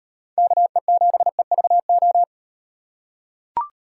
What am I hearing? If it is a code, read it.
Morse code, heard as a keyed tone, about 38 wpm, 700 Hz. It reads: KE7EVO